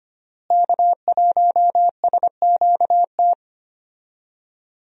Morse code, heard as a keyed tone, about 25 wpm, 700 Hz. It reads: K1SQT
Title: USA Call Signs - 5 Characters - Repeated 3x - 25wpm